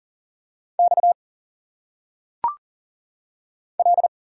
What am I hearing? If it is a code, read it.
Morse code, heard as a keyed tone, about 39 wpm, 700 Hz. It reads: X L
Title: Single Letters - Phonetic Pronunciation 40wpm